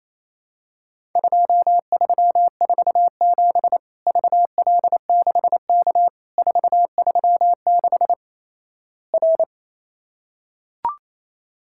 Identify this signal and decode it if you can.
Morse code, heard as a keyed tone, about 28 wpm, 700 Hz. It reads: 2347 VL6K 436 R
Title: Worked All Europe DX Contest - QTC Exchange - 28wpm